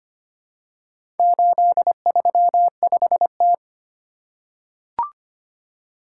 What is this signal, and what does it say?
Morse code, heard as a keyed tone, about 25 wpm, 700 Hz. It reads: 835T
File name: practice / 25wpm / Four Letter-Number Combinations - 25wpm